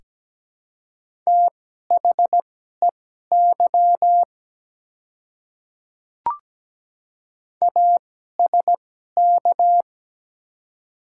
Morse code, heard as a keyed tone, about 17 wpm, spaced as for 12 wpm, 700 Hz. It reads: THEY ASK
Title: 2x Character Spacing - Top 200 Words - 17wpm